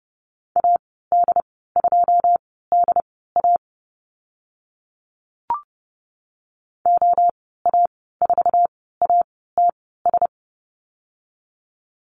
Morse code, heard as a keyed tone, about 32 wpm, spaced as for 17 wpm, 700 Hz. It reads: AD2DA OA4ATS
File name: Lesson 14g - Number 2 - Call Signs 30wpm